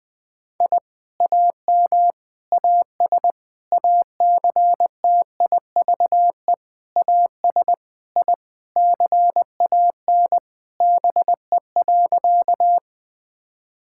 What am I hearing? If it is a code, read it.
Morse code, heard as a keyed tone, about 20 wpm, 700 Hz. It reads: I AM AS ACTIVE AS I CAN BE.
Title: Sentences from Top 1000 Words - Review Part 6 - 20wpm